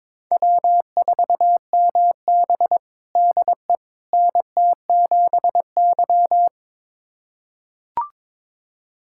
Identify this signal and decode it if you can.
Morse code, heard as a keyed tone, about 22 wpm, 700 Hz. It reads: W4MB DE NT7Y
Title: USA Callsign DE Callsign - 4 Characters - 22wpm